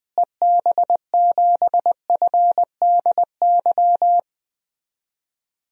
Morse code, heard as a keyed tone, about 20 wpm, 700 Hz. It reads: EB7FDY
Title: DXCC-100 Call Signs - 6 Characters - Repeated 3x - 20wpm